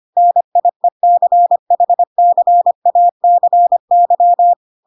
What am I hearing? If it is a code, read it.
Morse code, heard as a keyed tone, about 25 wpm, 700 Hz. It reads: NIECHCACY